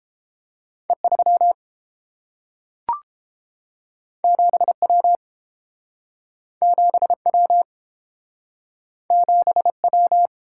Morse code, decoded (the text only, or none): E 3 7 W 7 W 7 W